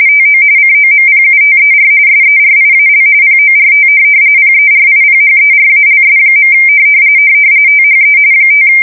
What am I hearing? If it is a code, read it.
RTTY, about 45.45 baud. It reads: TO COMPUTER SERIAL PORT PERIPHERALS. THE 5
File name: fsk170 / teletype-RTTY